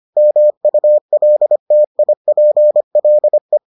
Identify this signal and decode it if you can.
Morse code, heard as a keyed tone, about 25 wpm, 600 Hz. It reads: MULTIPLE